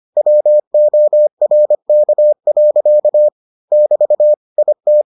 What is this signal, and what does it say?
Morse code, heard as a keyed tone, about 25 wpm, 600 Hz. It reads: WORK. <BT> IT